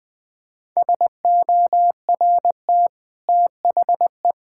Morse code, heard as a keyed tone, about 20 wpm, 700 Hz. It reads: SORT THE
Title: Easy Sentences with Hints - 20wpm